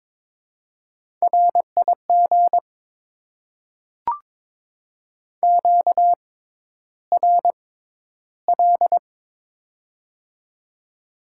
Morse code, heard as a keyed tone, about 22 wpm, spaced as for 11 wpm, 700 Hz. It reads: RIG QRL
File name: Top 20 QSO Elements - 6x Character Spacing - 22wpm